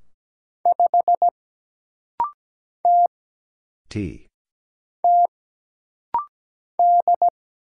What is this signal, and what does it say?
Morse code, heard as a keyed tone, about 17 wpm, 700 Hz. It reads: 5 T T D
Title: Single Letter-Number Alt - Rapid-Fire 17wpm